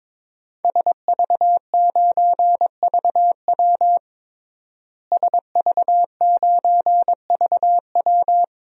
Morse code, heard as a keyed tone, about 22 wpm, 700 Hz. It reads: SV9VW SV9VW
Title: DXCC-100 Call Signs - 5 Characters - Repeated 3x - 22wpm